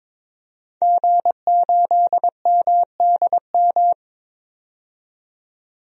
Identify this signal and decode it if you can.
Morse code, heard as a keyed tone, about 22 wpm, 700 Hz. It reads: G8MDM
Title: DXCC-100 Call Signs - 5 Characters - 22wpm